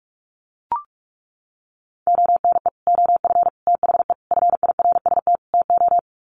Morse code, heard as a keyed tone, about 44 wpm, 700 Hz. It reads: ONE OF THE FIRST TO
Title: Sets of 5 Words Encore 45wpm